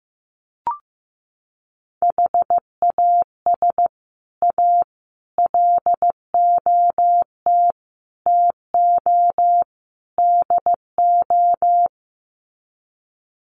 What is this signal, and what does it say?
Morse code, heard as a keyed tone, about 15 wpm, 700 Hz. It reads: HAS A LOT TO DO
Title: Sets of 5 Words Encore2 15wpm